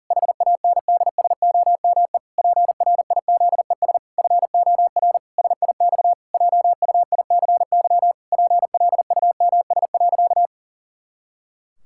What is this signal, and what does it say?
Morse code, heard as a keyed tone, about 40 wpm, 700 Hz. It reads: HANDSOME PRIZES FOR SIX JUICY PLUMS.
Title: Pangram Challenge v2 - 40wpm